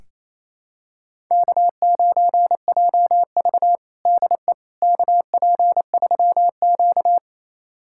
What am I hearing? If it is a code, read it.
Morse code, heard as a keyed tone, about 28 wpm, 700 Hz. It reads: K9JV DE KP3Q
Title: USA Callsign DE Callsign - 4 Characters - 28wpm